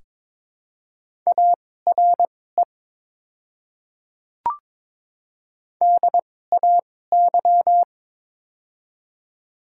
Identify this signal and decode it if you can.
Morse code, heard as a keyed tone, about 22 wpm, spaced as for 16 wpm, 700 Hz. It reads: ARE DAY
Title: 2x Character Spacing - Top 100 Words - 22wpm